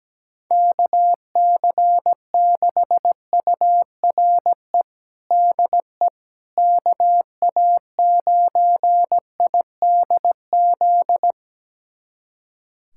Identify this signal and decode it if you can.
Morse code, heard as a keyed tone, about 17 wpm, 700 Hz. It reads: KC6URE DE KA9IDZ
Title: USA Callsign DE Callsign - 6 Characters - 17wpm